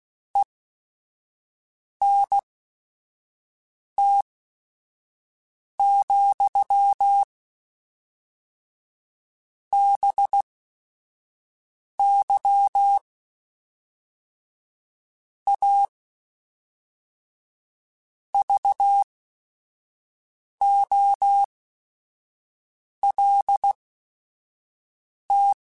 Morse code, heard as a keyed tone, about 16 wpm, spaced as for 5 wpm, 800 Hz. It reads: ENT, BY A VOLT
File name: Technical Subjects / Technical Subjects-mp3s / XL @5